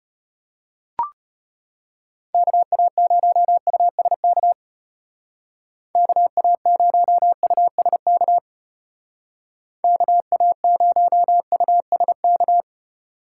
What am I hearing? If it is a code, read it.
Morse code, heard as a keyed tone, about 32 wpm, 700 Hz. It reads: KA0USK KA0USK KA0USK